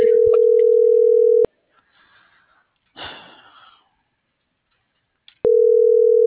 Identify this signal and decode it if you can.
Ringback tone (North America), 440 + 480 Hz.